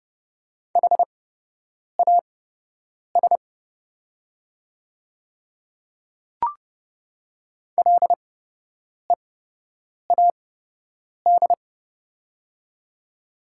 Morse code, heard as a keyed tone, about 29 wpm, spaced as for 8 wpm, 690 Hz. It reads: HAS LEAD
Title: 8x Character Spacing - Top 500 Words - 30wpm